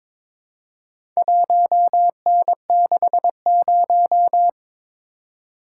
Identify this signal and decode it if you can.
Morse code, heard as a keyed tone, about 22 wpm, 700 Hz. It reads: 1N60
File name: practice / 22wpm / Four Letter-Number Combinations - 22wpm